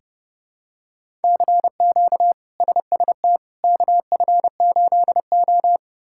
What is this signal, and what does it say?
Morse code, heard as a keyed tone, about 31 wpm, 700 Hz. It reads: CQ SST KF8O
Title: K1USN SST QSO Practice - 30wpm